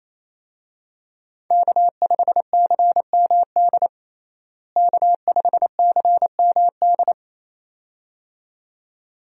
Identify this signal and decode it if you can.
Morse code, heard as a keyed tone, about 28 wpm, 700 Hz. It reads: K5CMD K5CMD
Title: USA Call Signs - 5 Characters - Repeated 2x - 28wpm